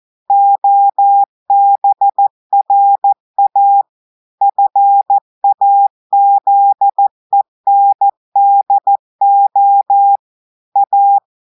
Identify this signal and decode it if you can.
Morse code, heard as a keyed tone, about 14 wpm, 800 Hz. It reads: OBRA FAZENDO A
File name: audios / portugais / portugais texte en clair 1 vitesse 14 mots minutes